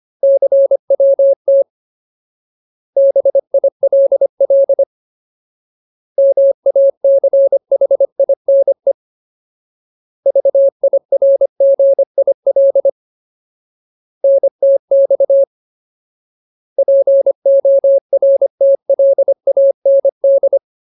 Morse code, heard as a keyed tone, about 25 wpm, 550 Hz. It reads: CWT BILL MACHINE VIRGIL NTX PORTLAND
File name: Ultimate Word List / 25WPM Ultimate